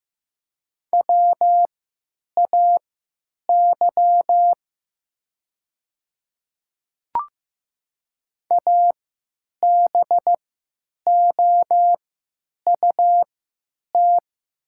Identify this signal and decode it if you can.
Morse code, heard as a keyed tone, about 15 wpm, spaced as for 9 wpm, 700 Hz. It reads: WAY ABOUT
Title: 3x Character Spacing - Top 200 Words - 15wpm